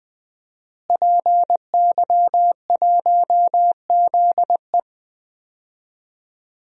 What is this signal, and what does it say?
Morse code, heard as a keyed tone, about 20 wpm, 700 Hz. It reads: PY1ZE
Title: DXCC-100 Call Signs - 5 Characters - Repeated 3x - 20wpm